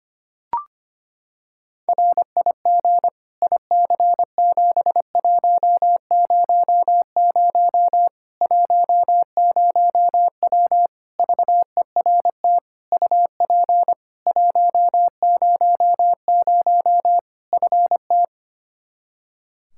Morse code, heard as a keyed tone, about 25 wpm, 700 Hz. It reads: RIG IC7100 10W VERT UP 100 FT
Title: Practice QSO - Exchange 2 - Info - 25wpm